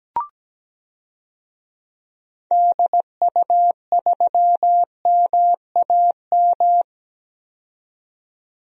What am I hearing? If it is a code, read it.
Morse code, heard as a keyed tone, about 17 wpm, 700 Hz. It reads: DU3MAM
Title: DXCC-100 Call Signs - 6 Characters - 17wpm